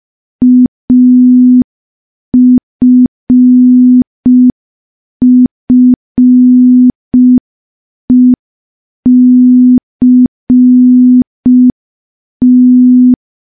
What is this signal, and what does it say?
Morse code, heard as a keyed tone, about 5 wpm, 250 Hz. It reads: AFFECT
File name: morse / morse-training-05WPM